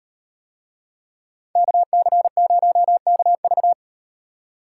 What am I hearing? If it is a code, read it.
Morse code, heard as a keyed tone, about 38 wpm, 700 Hz. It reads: KC0KV